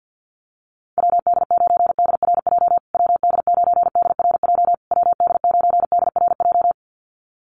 Morse code, heard as a keyed tone, about 49 wpm, 700 Hz. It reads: WD9DRJ WD9DRJ WD9DRJ
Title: USA Call Signs Repeated 3x 50wpm